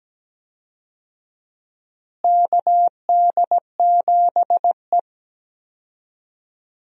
Morse code, heard as a keyed tone, about 17 wpm, 700 Hz. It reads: KD7E
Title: USA Call Signs - 4 Characters - Repeated 2x - 17wpm